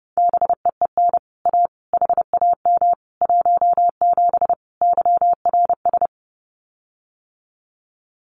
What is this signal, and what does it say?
Morse code, heard as a keyed tone, about 30 wpm, 700 Hz. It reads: BEEN A HAM 17 YRS
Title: Common QSO Phrases Encore 30wpm